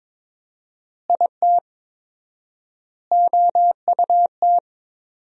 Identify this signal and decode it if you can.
Morse code, heard as a keyed tone, about 22 wpm, 700 Hz. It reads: IT OUT
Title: 4x Word-Spacing - Sentences from Top 200 Words - 22wpm